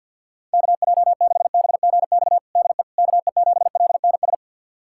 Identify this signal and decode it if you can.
Morse code, heard as a keyed tone, about 50 wpm, 700 Hz. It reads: KJ6BGX DE KE7LNS
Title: USA Callsign DE Callsign - 6 Characters - 50wpm